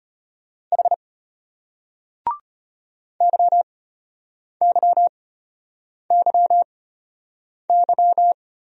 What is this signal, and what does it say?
Morse code, heard as a keyed tone, about 33 wpm, spaced as for 8 wpm, 700 Hz. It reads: H YYYY